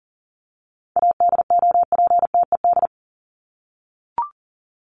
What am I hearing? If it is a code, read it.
Morse code, heard as a keyed tone, about 39 wpm, 700 Hz. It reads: ADOPTED
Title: ED Suffix - Top 100 Words with Base Words 40wpm